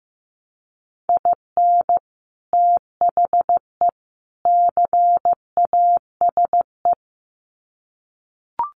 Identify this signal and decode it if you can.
Morse code, heard as a keyed tone, about 15 wpm, 700 Hz. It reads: IN THE CASE
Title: Sets of 3 Words 15wpm